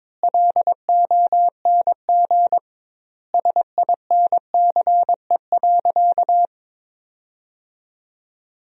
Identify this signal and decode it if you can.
Morse code, heard as a keyed tone, about 22 wpm, 700 Hz. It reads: LONG SINCE.
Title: 2x Word-Spacing - Sentences from Top 300 Words - 22wpm